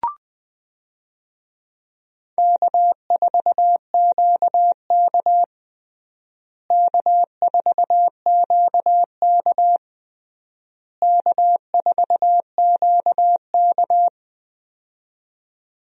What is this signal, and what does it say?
Morse code, heard as a keyed tone, about 20 wpm, 700 Hz. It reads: K4QK K4QK K4QK
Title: USA Call Signs - 4 Characters - Repeated 3x - 20wpm